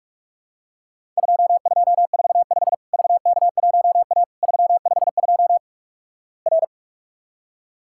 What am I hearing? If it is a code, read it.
Morse code, heard as a keyed tone, about 44 wpm, 700 Hz. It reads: 2245 VK1A 352 R
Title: Worked All Europe DX Contest - QTC Exchange - 45wpm